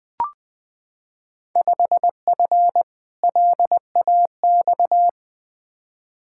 Morse code, heard as a keyed tone, about 20 wpm, 700 Hz. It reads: 5F LAX
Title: Field Day Exchanges Encore - 20wpm